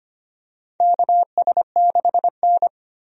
Morse code, heard as a keyed tone, about 25 wpm, 700 Hz. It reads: KS6N
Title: USA Call Signs - 4 Characters - 25wpm